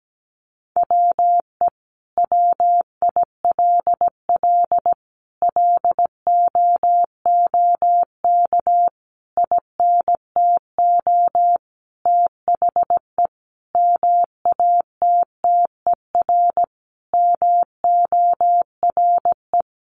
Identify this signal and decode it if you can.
Morse code, heard as a keyed tone, about 17 wpm, 700 Hz. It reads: WE WILL LOOK INTO THE MATTER MORE
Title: Sentences from Top 1400 Words 17wpm